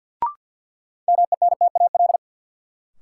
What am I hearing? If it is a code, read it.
Morse code, heard as a keyed tone, about 51 wpm, 700 Hz. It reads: MENTAL
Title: Sentences from Top 1000 Words - Review Part 4 - New Word Drill - 50wpm